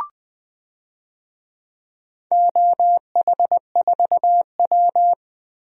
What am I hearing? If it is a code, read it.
Morse code, heard as a keyed tone, about 20 wpm, 700 Hz. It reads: OH4W
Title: DXCC-100 Call Signs - 4 Characters - 20wpm